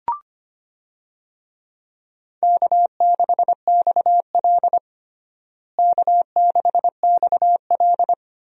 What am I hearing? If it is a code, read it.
Morse code, heard as a keyed tone, about 25 wpm, 700 Hz. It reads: K6XL K6XL
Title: USA Call Signs - 4 Characters - Repeated 2x - 25wpm